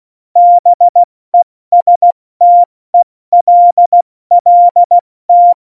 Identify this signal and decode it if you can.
Morse code, heard as a keyed tone, about 16 wpm, 700 Hz. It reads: BESTELLT
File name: Morsekurs / Witze / Witz 04. September 2018, 70 F80